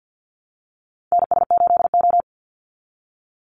Morse code, heard as a keyed tone, about 50 wpm, 700 Hz. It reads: NS8O
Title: Call Signs Repeated 2x - 50wpm